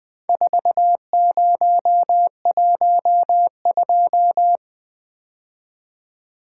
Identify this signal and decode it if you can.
Morse code, heard as a keyed tone, about 20 wpm, 700 Hz. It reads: 4012
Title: Four Letter-Number Combinations - 20wpm